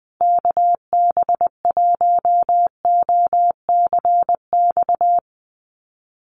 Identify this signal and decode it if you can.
Morse code, heard as a keyed tone, about 19 wpm, 700 Hz. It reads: KB1OCX
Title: USA Call Signs Repeated 3x 20wpm